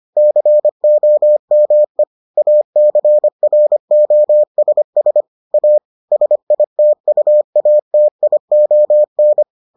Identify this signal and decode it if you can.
Morse code, heard as a keyed tone, about 25 wpm, 600 Hz. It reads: COME ACROSS A SITUATION